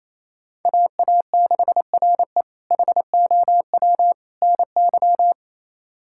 Morse code, heard as a keyed tone, about 27 wpm, 700 Hz. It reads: AA6RE HOW NY
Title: K1USN SST QSO Practice - 28wpm